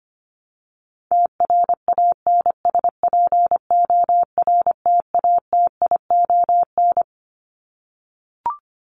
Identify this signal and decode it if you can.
Morse code, heard as a keyed tone, about 25 wpm, 700 Hz. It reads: TRANSPORTATION
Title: Top 1501-1750 Words 25wpm